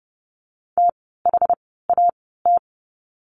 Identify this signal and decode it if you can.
Morse code, heard as a keyed tone, about 30 wpm, spaced as for 17 wpm, 700 Hz. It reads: THAT
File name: course / 30:10wpm / 108 Lesson 18f - Letter m - Words 30wpm